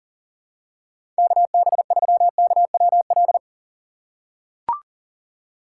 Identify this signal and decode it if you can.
Morse code, heard as a keyed tone, about 40 wpm, 700 Hz. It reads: KB3KWL
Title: USA Call Signs - 6 Characters - 6x Character Spacing - 40wpm